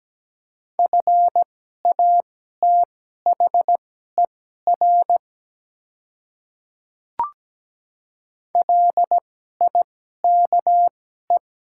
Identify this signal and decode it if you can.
Morse code, heard as a keyed tone, about 17 wpm, spaced as for 12 wpm, 700 Hz. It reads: FATHER LIKE
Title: 2x Character Spacing - Top 300 Words - 17wpm